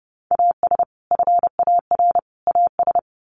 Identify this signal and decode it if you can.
Morse code, heard as a keyed tone, about 30 wpm, 700 Hz. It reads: AS FAR AS
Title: Sets of 3 Words 30wpm